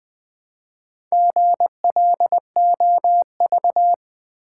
Morse code, heard as a keyed tone, about 20 wpm, 700 Hz. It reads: GLOV
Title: Build A Word - 1 Letter At A Time - 3 to 5-Letter Words - 20wpm